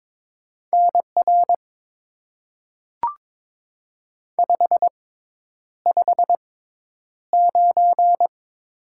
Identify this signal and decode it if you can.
Morse code, heard as a keyed tone, about 22 wpm, spaced as for 10 wpm, 700 Hz. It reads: NR 559
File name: Top 90 QSO Elements - 6x Character Spacing - 22wpm